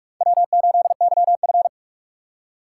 Morse code, heard as a keyed tone, about 43 wpm, 700 Hz. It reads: W8YF
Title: USA Call Signs - 4 Characters - 3x Character Spacing - 45wpm